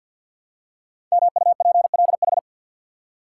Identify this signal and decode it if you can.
Morse code, heard as a keyed tone, about 50 wpm, 700 Hz. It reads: MUPLH
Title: Code Groups - 5 Letters - 50wpm